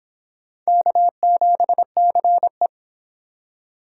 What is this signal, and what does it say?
Morse code, heard as a keyed tone, about 25 wpm, 700 Hz. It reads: K7CE